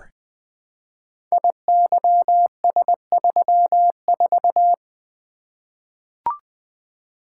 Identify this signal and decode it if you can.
Morse code, heard as a keyed tone, about 20 wpm, 700 Hz. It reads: IYS34
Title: Code Groups - 5 Letter:Numbers - 20wpm